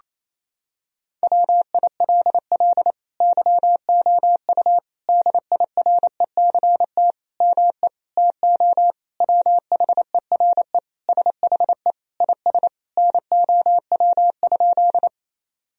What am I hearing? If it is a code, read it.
Morse code, heard as a keyed tone, about 28 wpm, 700 Hz. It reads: WILL YOU DIRECT ME TO WHERE SHE IS NOW?
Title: Sentences from Top 1000 Words - Review Part 2 - 28wpm